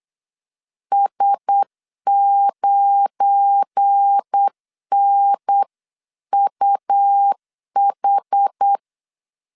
Morse code, heard as a keyed tone, about 8 wpm, 800 Hz. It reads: S9NUH
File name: CW / DET / 1 9 1500C S B 8wpm 5624